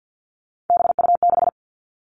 Morse code, heard as a keyed tone, about 50 wpm, 700 Hz. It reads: BU6
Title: Three Challenging Letter-Number Combinations 50wpm